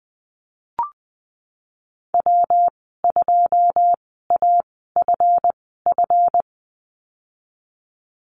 Morse code, heard as a keyed tone, about 20 wpm, spaced as for 15 wpm, 700 Hz. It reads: W2AFF
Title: 167 Lesson 27g - Number 7 - Call Signs 20wpm